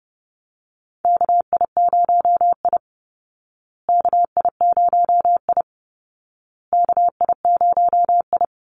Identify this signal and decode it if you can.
Morse code, heard as a keyed tone, about 30 wpm, 700 Hz. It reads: KI0I KI0I KI0I
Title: Call Signs Repeated 2x - 30wpm